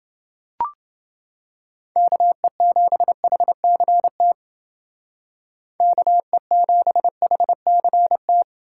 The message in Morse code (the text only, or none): KE7HCT KE7HCT